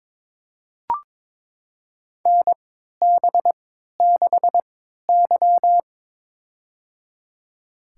Morse code, heard as a keyed tone, about 22 wpm, spaced as for 13 wpm, 700 Hz. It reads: NB6Y
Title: USA Call Signs - 4 Characters - 3x Character Spacing - 22wpm